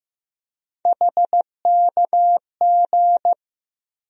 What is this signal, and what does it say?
Morse code, heard as a keyed tone, about 16 wpm, 700 Hz. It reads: HKG
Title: USA Call Sign Postfixes - 3 Letters 15wpm